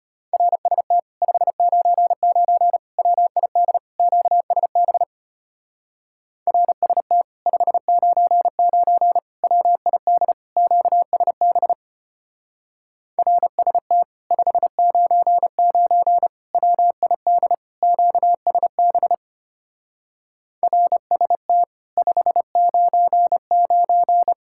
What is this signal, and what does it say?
Morse code, heard as a keyed tone, about 33 wpm, 700 Hz. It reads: RST 599 WID QSB RST 599 WID QSB RST 599 WID QSB RST 599